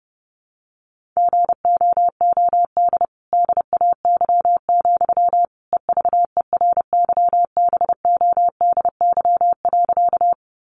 Morse code, heard as a keyed tone, about 30 wpm, 700 Hz. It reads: GOOD DAY, EVERYBODY.
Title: Sentences from Top 1400 Words 30wpm